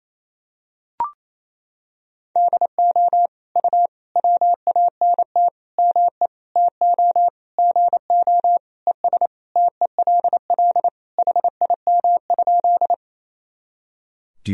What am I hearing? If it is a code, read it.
Morse code, heard as a keyed tone, about 27 wpm, 700 Hz. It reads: DO U WANT ME TO GO ES TELL HIM?
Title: Sentences from Top 200 Words and Abbreviations 28wpm